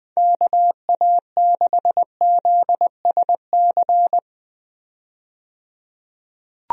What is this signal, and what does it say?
Morse code, heard as a keyed tone, about 21 wpm, 700 Hz. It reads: KA6ZSC
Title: USA Call Signs - 6 Characters - 20wpm